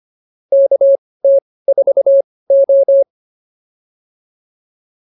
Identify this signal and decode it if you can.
Morse code, heard as a keyed tone, about 25 wpm, spaced as for 18 wpm, 550 Hz. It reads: KT4O